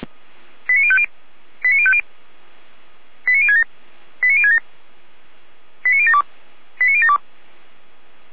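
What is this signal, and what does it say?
Five-tone selective call (ZVEI), heard as quick runs of five tones, tones of about 70 ms. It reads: 89050 89050 89067 89067 89072 89072
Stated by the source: ZVEI five-tone selective call 89022 received on SDR, fire brigade Feuerwehr Orlamünde.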